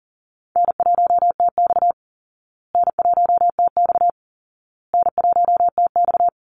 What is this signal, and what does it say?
Morse code, heard as a keyed tone, about 40 wpm, 700 Hz. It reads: N1TX N1TX N1TX
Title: Easy Call Signs Repeated 3x - 40wpm